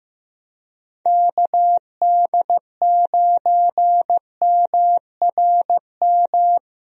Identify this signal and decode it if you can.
Morse code, heard as a keyed tone, about 16 wpm, 700 Hz. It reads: KD9MRM
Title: USA Call Signs - 6 Characters - Repeated 2x - 15wpm